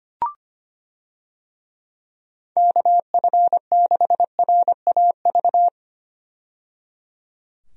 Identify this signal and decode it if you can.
Morse code, heard as a keyed tone, about 25 wpm, 700 Hz. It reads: KF6RAV
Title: USA Call Signs - 6 Characters - 25wpm